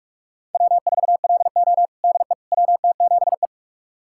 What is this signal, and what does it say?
Morse code, heard as a keyed tone, about 41 wpm, 700 Hz. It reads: W4LO DE WT7E
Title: USA Callsign DE Callsign - 4 Characters - 45wpm